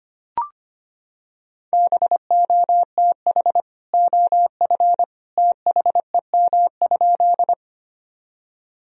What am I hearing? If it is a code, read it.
Morse code, heard as a keyed tone, about 25 wpm, 700 Hz. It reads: BOTH OF THEM?
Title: Sentences from Top 1000 Words - Review Part 1 - 25wpm